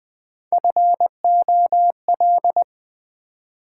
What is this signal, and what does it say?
Morse code, heard as a keyed tone, about 20 wpm, 700 Hz. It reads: FOL
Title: Build A Word - 1 Letter At A Time - 4-Letter Words - 20wpm